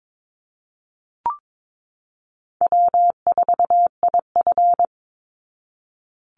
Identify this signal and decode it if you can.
Morse code, heard as a keyed tone, about 22 wpm, 700 Hz. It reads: W4IF
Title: Easy Call Signs Repeated 3x - 22wpm